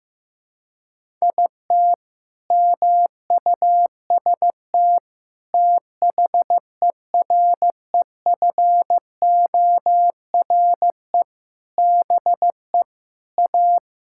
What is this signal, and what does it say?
Morse code, heard as a keyed tone, about 15 wpm, 700 Hz. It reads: IT MUST THEREFORE BE A
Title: Sentences from Top 1000 Words - Review Part 6 - 15wpm